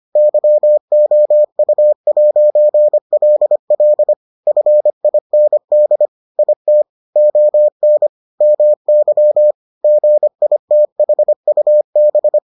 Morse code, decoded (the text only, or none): YOU'LL FIND IT ON MY GITHUB